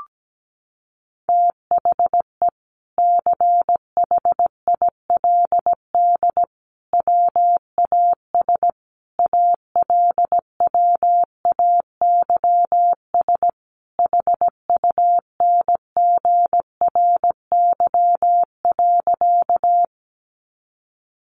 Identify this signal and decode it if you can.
Morse code, heard as a keyed tone, about 17 wpm, 700 Hz. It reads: THE CHILD WAS ALWAYS HUNGRY.
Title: Sentences from Top 1400 Words 17wpm